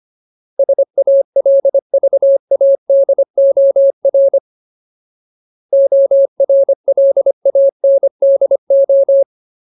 Morse code, heard as a keyed tone, about 25 wpm, 550 Hz. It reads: SALVADOR ORLANDO
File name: Ultimate Word List / 25WPM Ultimate